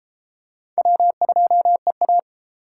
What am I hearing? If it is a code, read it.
Morse code, heard as a keyed tone, about 32 wpm, 700 Hz. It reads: W2EA